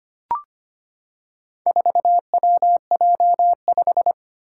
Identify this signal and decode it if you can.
Morse code, heard as a keyed tone, about 25 wpm, 700 Hz. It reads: 4WJ5
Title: Four Challenging Letter-Number Combinations 25wpm